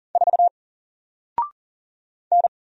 Morse code, heard as a keyed tone, about 40 wpm, 700 Hz. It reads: 4 N
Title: Single Character (Letter:Number:Punctuation:Prosign) Alt 40wpm